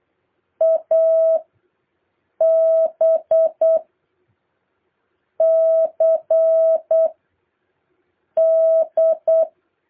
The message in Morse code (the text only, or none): AB CD